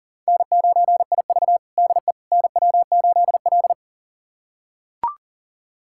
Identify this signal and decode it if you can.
Morse code, heard as a keyed tone, about 40 wpm, 700 Hz. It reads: N9IV DE NW8L
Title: USA Callsign DE Callsign - 4 Characters - 40wpm